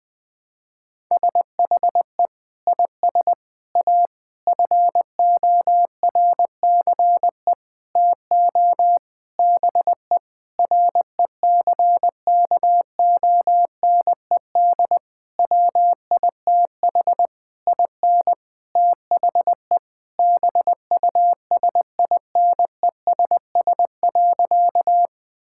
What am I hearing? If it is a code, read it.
Morse code, heard as a keyed tone, about 20 wpm, 700 Hz. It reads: SHE IS A FORCE TO BE RECKONED WITH IN THE BUSINESS.
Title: Sentences with 8-Word Phrases - 20wpm